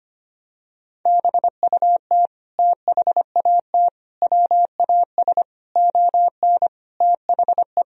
Morse code, heard as a keyed tone, about 25 wpm, 700 Hz. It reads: BUT THAT WAS ON THE
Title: Sentences from Top 1000 Words - Review Part 1 - 25wpm